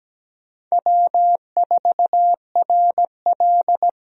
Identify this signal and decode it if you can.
Morse code, heard as a keyed tone, about 17 wpm, 700 Hz. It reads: W4RL
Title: USA Call Signs - 4 Characters - Repeated 2x - 17wpm